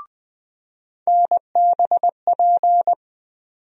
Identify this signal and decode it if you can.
Morse code, heard as a keyed tone, about 20 wpm, 700 Hz. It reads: NBP
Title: USA Call Sign Postfixes - 3 Letters 20wpm